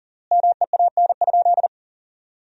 Morse code, heard as a keyed tone, about 40 wpm, 700 Hz. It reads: MEAN?
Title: 3x Word-Spacing - Sentences from Top 200 Words - 40wpm